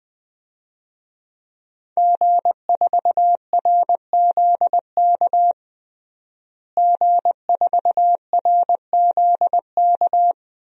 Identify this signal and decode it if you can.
Morse code, heard as a keyed tone, about 21 wpm, 700 Hz. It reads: G4RZK G4RZK
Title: DXCC-100 Call Signs - 5 Characters - Repeated 3x - 20wpm